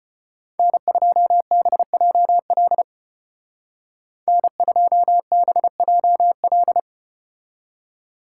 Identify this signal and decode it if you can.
Morse code, heard as a keyed tone, about 32 wpm, 700 Hz. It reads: N2BJL N2BJL